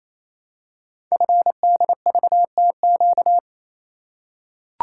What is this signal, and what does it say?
Morse code, heard as a keyed tone, about 27 wpm, 700 Hz. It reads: FDVTQ
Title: Code Groups - 5 Letters - 28wpm